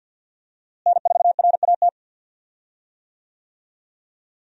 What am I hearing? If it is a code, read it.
Morse code, heard as a keyed tone, about 51 wpm, 700 Hz. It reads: N4RAT
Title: USA Call Signs - 5 Characters - Repeated 2x - 50wpm